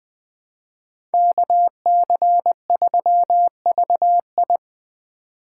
Morse code, heard as a keyed tone, about 21 wpm, 700 Hz. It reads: KC3VI